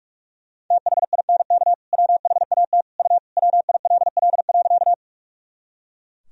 Morse code, heard as a keyed tone, about 43 wpm, 700 Hz. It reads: THINK WHAT U WILL.
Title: Sentences from Top 200 Words and Abbreviations 45wpm